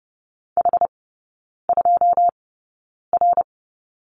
Morse code, heard as a keyed tone, about 30 wpm, spaced as for 9 wpm, 690 Hz. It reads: H2R
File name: Lesson 16e - Letter c - Three Characters 30wpm